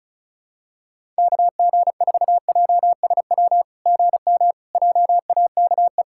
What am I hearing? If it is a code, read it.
Morse code, heard as a keyed tone, about 34 wpm, 690 Hz. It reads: KG4JSW GM JAKE
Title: SOTA QSO Practice 35wpm